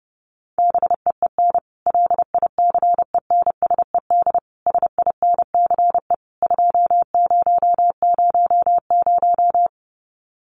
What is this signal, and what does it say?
Morse code, heard as a keyed tone, about 30 wpm, 700 Hz. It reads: BEEN LICENSED SINCE 2000
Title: Common QSO Phrases 30wpm